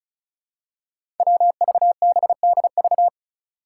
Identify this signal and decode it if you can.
Morse code, heard as a keyed tone, about 36 wpm, 700 Hz. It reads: WVBDV
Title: Five Challenging Letter-Number Combinations 35wpm